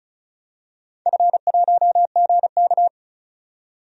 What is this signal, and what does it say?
Morse code, heard as a keyed tone, about 36 wpm, 700 Hz. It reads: F1GK
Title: DXCC-100 Call Signs - 4 Characters - 35wpm